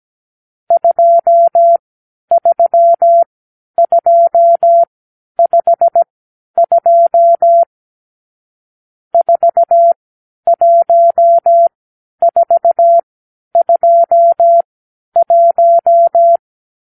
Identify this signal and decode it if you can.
Morse code, heard as a keyed tone, about 17 wpm, spaced as for 10 wpm, 680 Hz. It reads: 23252 41421
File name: arrl / Review (1-5)